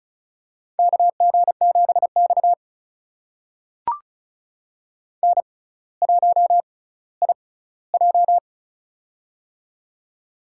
Morse code, heard as a keyed tone, about 34 wpm, spaced as for 18 wpm, 700 Hz. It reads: KG7X N1IJ
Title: USA Call Signs - 4 Characters - 6x Character Spacing - 35wpm